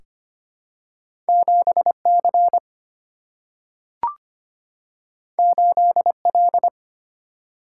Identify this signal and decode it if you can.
Morse code, heard as a keyed tone, about 25 wpm, 700 Hz. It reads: 7C 8L
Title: Single-Letter Single-Number 25wpm